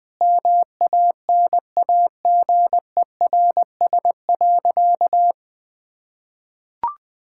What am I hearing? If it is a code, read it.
Morse code, heard as a keyed tone, about 20 wpm, 700 Hz. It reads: MANAGERS.
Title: Sentences with 6-Word Phrases - 20wpm